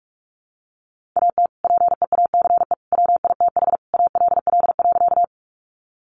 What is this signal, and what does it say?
Morse code, heard as a keyed tone, about 44 wpm, 700 Hz. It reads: AT PEACE WITH ALL.
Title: Sentences from Top 1500 Words 45wpm